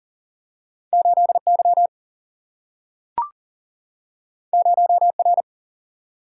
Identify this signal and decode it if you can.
Morse code, heard as a keyed tone, about 40 wpm, 700 Hz. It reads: 8Y 0R
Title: Single-Letter Single-Number 40wpm